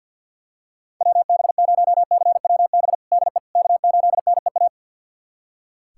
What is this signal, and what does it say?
Morse code, heard as a keyed tone, about 47 wpm, 700 Hz. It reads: WB0CWB DE K8NEA
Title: USA Callsign DE Callsign 50wpm